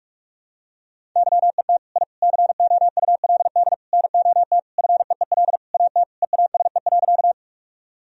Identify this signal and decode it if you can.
Morse code, heard as a keyed tone, about 46 wpm, 700 Hz. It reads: YET I COULD NOT FEEL AT EASE.
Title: Sentences from Top 1000 Words - Review Part 1 - 45wpm